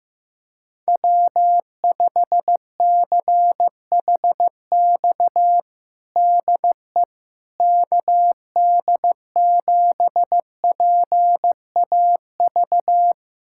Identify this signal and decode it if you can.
Morse code, heard as a keyed tone, about 15 wpm, 700 Hz. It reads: W5CHX DE KD7PAV
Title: USA Callsign DE Callsign 15wpm